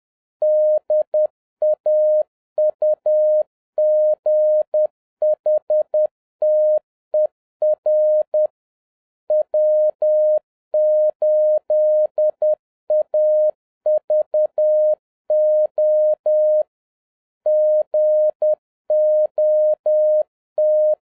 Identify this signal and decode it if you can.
Morse code, heard as a keyed tone, about 10 wpm, 610 Hz. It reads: DAUGHTER W8AVO GOT